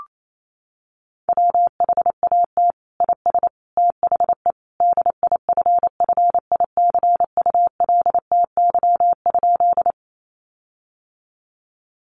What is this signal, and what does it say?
Morse code, heard as a keyed tone, about 28 wpm, 700 Hz. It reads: WHAT IS THE DIFFICULTY?
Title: Sentences from Top 1500 Words 28wpm